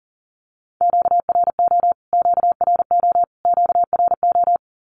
Morse code, heard as a keyed tone, about 40 wpm, 700 Hz. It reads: QRO QRO QRO
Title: Q-Codes Repeated 3x 40wpm